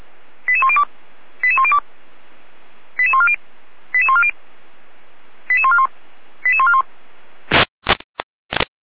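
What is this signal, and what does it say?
Five-tone selective call (ZVEI), heard as quick runs of five tones, tones of about 70 ms. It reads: 88102 88102 88140 88140 88141 88141
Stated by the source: ZVEI five-tone selective call received on SDR, code 88106.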